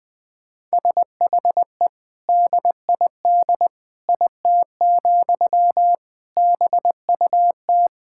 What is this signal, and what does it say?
Morse code, heard as a keyed tone, about 20 wpm, 700 Hz. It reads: SHE DID IT, BUT
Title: Sentences from Top 1000 Words - Review Part 6 - 20wpm